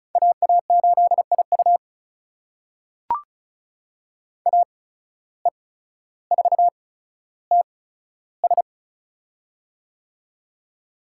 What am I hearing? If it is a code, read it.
Morse code, heard as a keyed tone, about 36 wpm, spaced as for 15 wpm, 700 Hz. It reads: AA8IU AE4TS
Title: USA Call Signs - 5 Characters - 8x Character Spacing - 35wpm